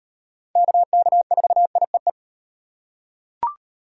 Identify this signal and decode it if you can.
Morse code, heard as a keyed tone, about 37 wpm, 700 Hz. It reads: KK4IEE